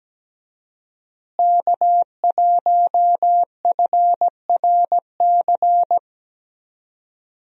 Morse code, heard as a keyed tone, about 17 wpm, 700 Hz. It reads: K1FRC
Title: USA Call Signs - 5 Characters - Repeated 2x - 17wpm